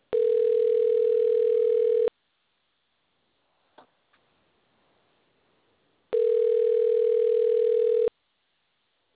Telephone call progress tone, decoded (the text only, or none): ringback tone